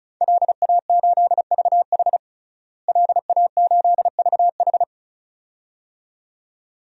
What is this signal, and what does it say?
Morse code, heard as a keyed tone, about 36 wpm, 700 Hz. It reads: LA8VH LA8VH
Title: DXCC-100 Call Signs - 5 Characters - Repeated 2x - 35wpm